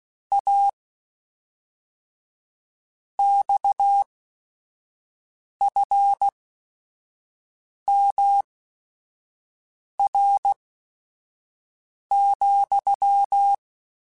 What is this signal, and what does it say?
Morse code, heard as a keyed tone, about 16 wpm, spaced as for 5 wpm, 800 Hz. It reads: A XFMR,
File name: Technical Subjects / Technical Subjects-mp3s / Sw Pwr Sup @5